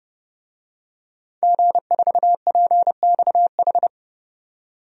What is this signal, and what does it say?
Morse code, heard as a keyed tone, about 29 wpm, 700 Hz. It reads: G4PXH
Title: DXCC-100 Call Signs - 5 Characters - 30wpm